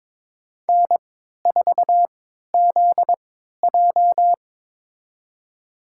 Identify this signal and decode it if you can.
Morse code, heard as a keyed tone, about 22 wpm, spaced as for 13 wpm, 700 Hz. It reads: N4ZJ
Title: USA Call Signs - 4 Characters - 3x Character Spacing - 22wpm